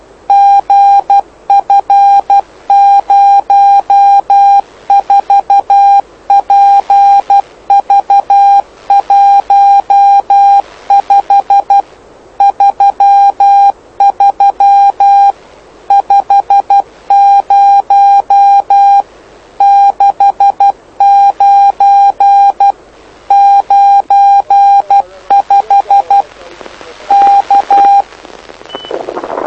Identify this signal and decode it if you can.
Morse code, heard as a keyed tone, about 12 wpm, 800 Hz. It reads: GF04PV15 33 50 69 95 K
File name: lu4aao / lu7aa 09dic2011 exp planeador vuelo1